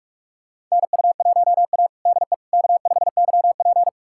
Morse code, heard as a keyed tone, about 46 wpm, 700 Hz. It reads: NU1A DE K5YP
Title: USA Callsign DE Callsign - 4 Characters - 45wpm